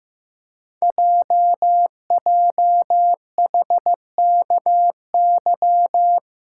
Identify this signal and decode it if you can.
Morse code, heard as a keyed tone, about 15 wpm, 700 Hz. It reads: JJHKY
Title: Code Groups - 5 Letters - 15wpm